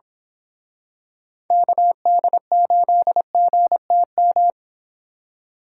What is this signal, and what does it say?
Morse code, heard as a keyed tone, about 27 wpm, 700 Hz. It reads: KD8GTM
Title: Speed Racing - USA 2x3 Call Signs 22wpm